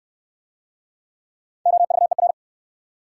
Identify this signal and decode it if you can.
Morse code, heard as a keyed tone, about 60 wpm, 700 Hz. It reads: OVER